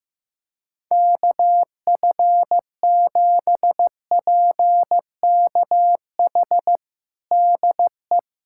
Morse code, heard as a keyed tone, about 15 wpm, 700 Hz. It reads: KF7PKH DE
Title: USA Callsign DE Callsign 15wpm